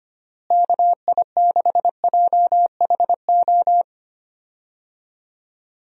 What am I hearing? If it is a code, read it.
Morse code, heard as a keyed tone, about 25 wpm, 700 Hz. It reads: KI6JHO
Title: USA Call Signs - 6 Characters - Repeated 2x - 25wpm